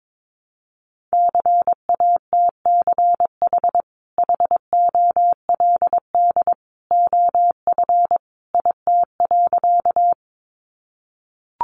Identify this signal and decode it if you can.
Morse code, heard as a keyed tone, about 22 wpm, 700 Hz. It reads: CATCH HOLD OF IT.
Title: Sentences from Top 1500 Words 22wpm